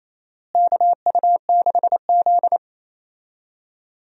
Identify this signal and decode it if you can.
Morse code, heard as a keyed tone, about 28 wpm, 700 Hz. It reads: KU6Z
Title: USA Call Signs - 4 Characters - Repeated 3x - 28wpm